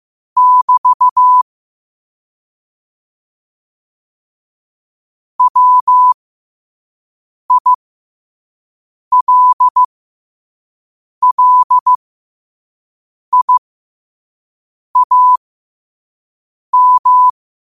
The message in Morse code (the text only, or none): <BT> WILLIAM